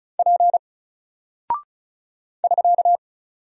Morse code, heard as a keyed tone, about 33 wpm, 700 Hz. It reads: P <SK>
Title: Single Character (Letter:Number:Punctuation:Prosign) Alt 35wpm